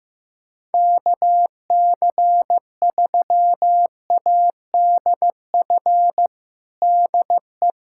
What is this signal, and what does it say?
Morse code, heard as a keyed tone, about 15 wpm, 700 Hz. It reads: KC3ADF DE
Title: USA Callsign DE Callsign 15wpm